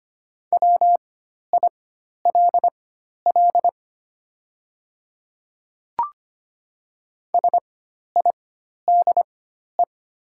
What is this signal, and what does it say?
Morse code, heard as a keyed tone, about 25 wpm, spaced as for 12 wpm, 690 Hz. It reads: WILL SIDE
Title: 4x Character Spacing - Top 200 Words - 25wpm